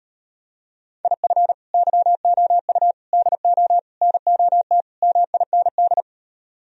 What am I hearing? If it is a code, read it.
Morse code, heard as a keyed tone, about 39 wpm, 700 Hz. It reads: IF YOU DO NOT MIND